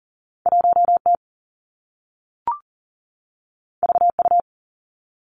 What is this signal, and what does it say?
Morse code, heard as a keyed tone, about 41 wpm, 700 Hz. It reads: 1 T V U
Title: Two Challenging Letter-Number Combinations 40wpm